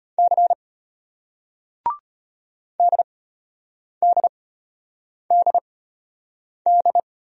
Morse code, heard as a keyed tone, about 34 wpm, spaced as for 8 wpm, 700 Hz. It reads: C DDDD